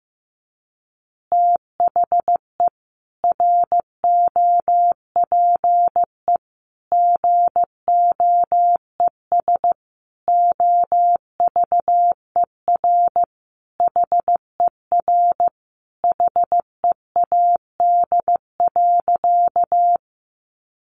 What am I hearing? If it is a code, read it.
Morse code, heard as a keyed tone, about 15 wpm, 700 Hz. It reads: THE ROPE GOES OVER HER HEAD.
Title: Easy Phrases 15wpm